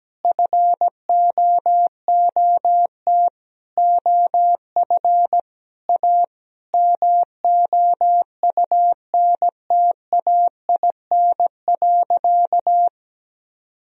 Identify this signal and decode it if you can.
Morse code, heard as a keyed tone, about 17 wpm, 700 Hz. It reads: FOOT OF A MOUNTAIN.
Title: Sentences from Top 1000 Words - Review Part 3 - 17wpm